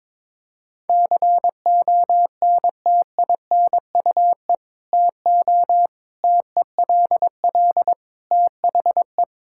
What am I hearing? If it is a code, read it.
Morse code, heard as a keyed tone, about 22 wpm, 700 Hz. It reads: CONTINUE TO TELL THE
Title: Sentences from Top 1000 Words - Review Part 6 - 22wpm